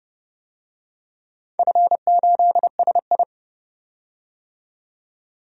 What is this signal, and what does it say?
Morse code, heard as a keyed tone, about 30 wpm, 700 Hz. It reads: F8SI
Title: DXCC-100 Call Signs - 4 Characters - 30wpm